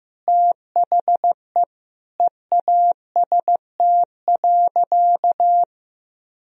Morse code, heard as a keyed tone, about 15 wpm, 700 Hz. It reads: THE EAST.
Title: Sentences from Top 1000 Words - Review Part 3 - 15wpm